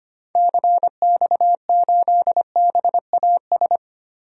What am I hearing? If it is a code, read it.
Morse code, heard as a keyed tone, about 25 wpm, 700 Hz. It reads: CX8BAS
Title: DXCC-100 Call Signs - 6 Characters - 25wpm